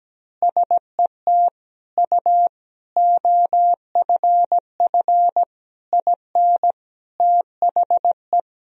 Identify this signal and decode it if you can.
Morse code, heard as a keyed tone, about 17 wpm, 700 Hz. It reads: SET U OFF IN THE
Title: Sentences from Top 200 Words and Abbreviations 17wpm